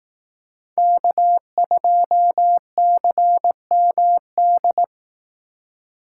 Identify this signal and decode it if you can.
Morse code, heard as a keyed tone, about 18 wpm, 700 Hz. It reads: K2CMD
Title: Speed Racing - USA 1x3 Call Signs 15wpm